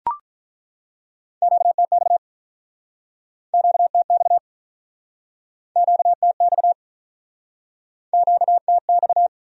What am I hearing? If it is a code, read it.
Morse code, heard as a keyed tone, about 48 wpm, 700 Hz. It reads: QTX QTX QTX QTX